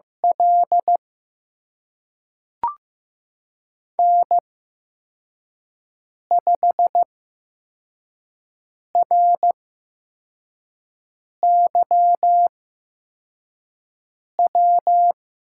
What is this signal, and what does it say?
Morse code, heard as a keyed tone, about 15 wpm, spaced as for 4 wpm, 700 Hz. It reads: L N5RYW
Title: USA Call Signs - 8x Character Spacing - 15wpm